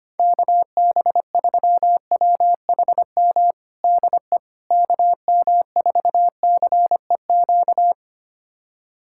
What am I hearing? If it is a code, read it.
Morse code, heard as a keyed tone, about 25 wpm, 700 Hz. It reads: KB3WHM DE KM4CEQ
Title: USA Callsign DE Callsign - 6 Characters - 25wpm